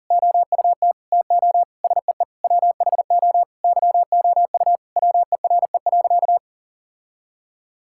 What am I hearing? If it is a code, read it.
Morse code, heard as a keyed tone, about 40 wpm, 700 Hz. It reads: OUT TO SEE WHO YOU WERE.
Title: Sentences from Top 1000 Words - Review Part 6 - 40wpm